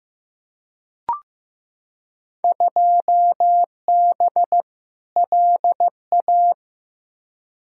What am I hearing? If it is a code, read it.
Morse code, heard as a keyed tone, about 15 wpm, 700 Hz. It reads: 2B LA
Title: Field Day Exchanges Encore - 15wpm